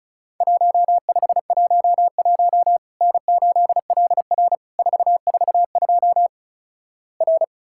Morse code, heard as a keyed tone, about 36 wpm, 700 Hz. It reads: 1511 N8LR 442 R
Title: Worked All Europe DX Contest - QTC Exchange - 35wpm